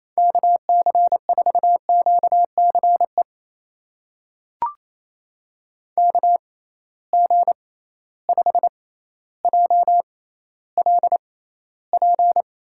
Morse code, heard as a keyed tone, about 27 wpm, spaced as for 14 wpm, 700 Hz. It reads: KC4QCE KG5JLP
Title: USA Call Signs - 6 Characters - 6x Character Spacing - 28wpm